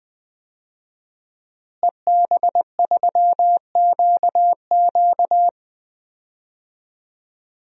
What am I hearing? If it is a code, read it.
Morse code, heard as a keyed tone, about 20 wpm, 700 Hz. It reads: EB3QQ
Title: DXCC-100 Call Signs - 5 Characters - Repeated 2x - 20wpm